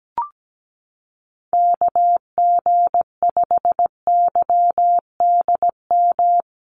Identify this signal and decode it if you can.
Morse code, heard as a keyed tone, about 17 wpm, 700 Hz. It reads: KG5YDM
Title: USA Call Signs Repeated 3x 17wpm